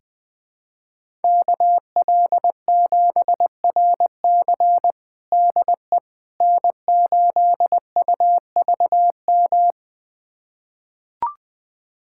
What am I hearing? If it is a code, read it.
Morse code, heard as a keyed tone, about 20 wpm, 700 Hz. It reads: KL7RC DE N8UVM
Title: USA Callsign DE Callsign - 5 Characters - 20wpm